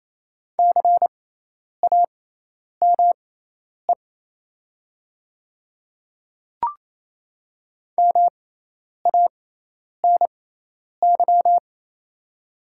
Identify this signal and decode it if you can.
Morse code, heard as a keyed tone, about 28 wpm, spaced as for 10 wpm, 700 Hz. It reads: CAME MANY
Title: 6x Character Spacing - Top 200 Words - 28wpm